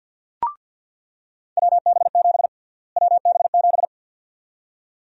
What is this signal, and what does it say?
Morse code, heard as a keyed tone, about 47 wpm, 700 Hz. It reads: WB7 WB7
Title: USA Call Sign Prefixes - Repeated 2x - 50wpm